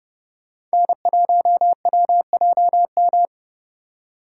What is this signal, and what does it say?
Morse code, heard as a keyed tone, about 29 wpm, 700 Hz. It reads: N1WJM